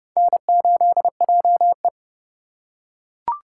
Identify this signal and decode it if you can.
Morse code, heard as a keyed tone, about 27 wpm, 700 Hz. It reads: N8JE